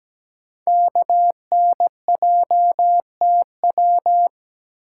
Morse code, heard as a keyed tone, about 17 wpm, 700 Hz. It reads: KNJTW
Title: Code Groups - 5 Letter:Numbers - 17wpm